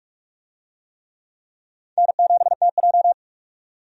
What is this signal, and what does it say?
Morse code, heard as a keyed tone, about 46 wpm, 700 Hz. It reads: N7TJ
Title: USA Call Signs - 4 Characters - 45wpm